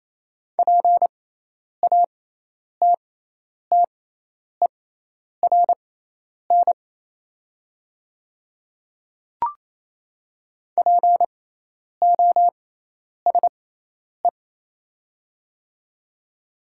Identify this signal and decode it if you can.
Morse code, heard as a keyed tone, about 28 wpm, spaced as for 10 wpm, 700 Hz. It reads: PATTERN POSE
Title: 6x Character Spacing - Top 500 Words - 28wpm